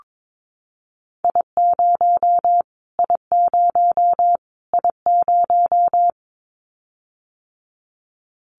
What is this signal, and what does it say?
Morse code, heard as a keyed tone, about 22 wpm, 700 Hz. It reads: I0 I0 I0
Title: Callsign Prefixes for Top 25 DXCC - Repeated 3x - 22wpm